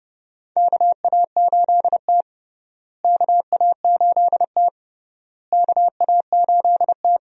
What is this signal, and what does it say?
Morse code, heard as a keyed tone, about 29 wpm, 700 Hz. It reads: KA8T KA8T KA8T
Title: USA Call Signs - 4 Characters - Repeated 3x - 30wpm